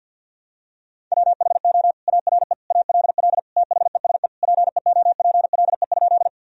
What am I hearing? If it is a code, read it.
Morse code, heard as a keyed tone, about 47 wpm, 700 Hz. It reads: WHO ARE ALL THESE PEOPLE?
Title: Sentences from Top 1000 Words - Review Part 1 - 50wpm